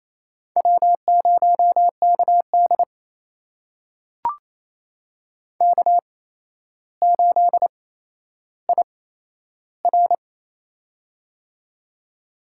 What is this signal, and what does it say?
Morse code, heard as a keyed tone, about 27 wpm, spaced as for 12 wpm, 700 Hz. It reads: W0KD K8IR
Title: USA Call Signs - 4 Characters - 8x Character Spacing - 28wpm